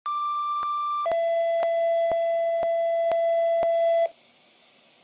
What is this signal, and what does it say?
Two-tone page, heard as a one-second tone, then a three-second tone, about 1180 then 670 Hz. It is heard from 0.1 to 4.1 s.